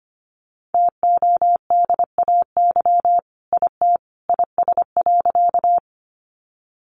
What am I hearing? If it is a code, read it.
Morse code, heard as a keyed tone, about 25 wpm, 700 Hz. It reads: TODAY IT IS.
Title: Sentences from Top 600 Words 25wpm